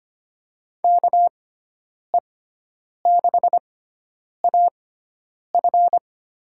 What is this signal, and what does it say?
Morse code, heard as a keyed tone, about 25 wpm, spaced as for 9 wpm, 700 Hz. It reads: KE6AF
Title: USA Call Signs - 6 Characters - 6x Character Spacing - 25wpm